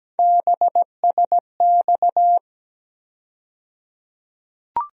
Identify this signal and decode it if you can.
Morse code, heard as a keyed tone, about 17 wpm, 700 Hz. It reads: BSX
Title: USA Call Sign Postfixes - 3 Letters 17wpm